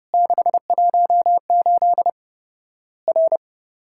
Morse code, heard as a keyed tone, about 30 wpm, 700 Hz. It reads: 618 R
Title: Worked All Europe DX Contest - QTC Exchange - 30wpm